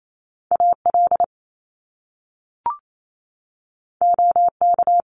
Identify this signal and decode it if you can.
Morse code, heard as a keyed tone, about 28 wpm, 700 Hz. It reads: AL OK
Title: Top Two Letter Combinations in Words 28wpm